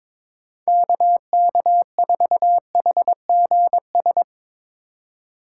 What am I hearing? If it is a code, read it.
Morse code, heard as a keyed tone, about 23 wpm, 700 Hz. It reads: KK4HGS
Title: Speed Racing - USA 2x3 Call Signs 22wpm